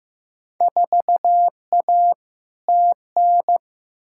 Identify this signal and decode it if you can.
Morse code, heard as a keyed tone, about 15 wpm, 700 Hz. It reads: 4A TN
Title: Field Day Exchanges - 15wpm